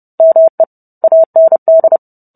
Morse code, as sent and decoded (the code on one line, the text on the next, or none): -- . / .- -. -..
ME AND